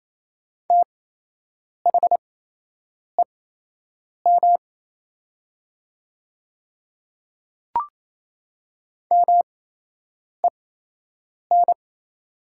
Morse code, heard as a keyed tone, about 28 wpm, spaced as for 8 wpm, 700 Hz. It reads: THEM MEN
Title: 8x Character Spacing - Top 200 Words - 28wpm